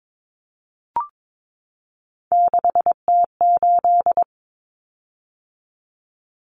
Morse code, heard as a keyed tone, about 22 wpm, 700 Hz. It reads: 6T8
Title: Common Cut Numbers - 3-Digit 22wpm